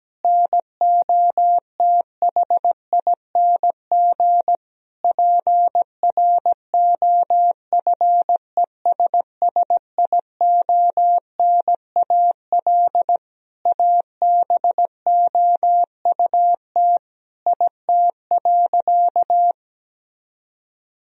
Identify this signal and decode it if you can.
Morse code, heard as a keyed tone, about 17 wpm, 700 Hz. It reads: NOTHING PROFESSIONAL ABOUT IT.
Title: Sentences from Top 1000 Words - Review Part 4 - 17wpm